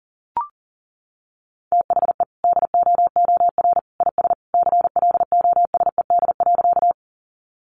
Morse code, heard as a keyed tone, about 40 wpm, 700 Hz. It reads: THE DOOR IS CLOSED.
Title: Sentences from Top 1400 Words 40wpm